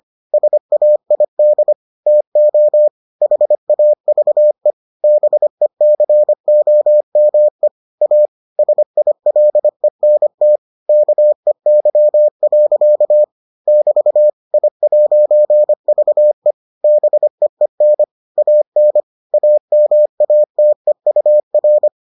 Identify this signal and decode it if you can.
Morse code, heard as a keyed tone, about 25 wpm, 600 Hz. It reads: SAID TO HAVE BECOME A SILENT KEY. <BT> I'VE BEEN AN AMATEUR